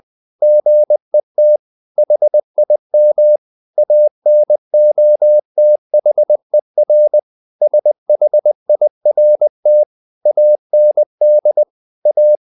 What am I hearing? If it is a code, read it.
Morse code, heard as a keyed tone, about 20 wpm, 600 Hz. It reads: GET HIM ANOTHER SHIRT AND A
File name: the old man and the sea / the old man and the sea part 1 20wpm